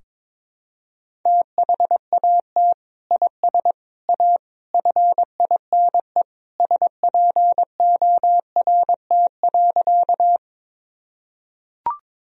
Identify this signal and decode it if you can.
Morse code, heard as a keyed tone, about 22 wpm, 700 Hz. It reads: THAT IS A FINE SPORT.
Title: Sentences from Top 1000 Words - Review Part 4 - 22wpm